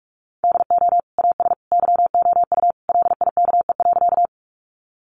Morse code, heard as a keyed tone, about 44 wpm, 700 Hz. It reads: DO AS YOU LIKE.
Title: Sentences from Top 100 Words 45wpm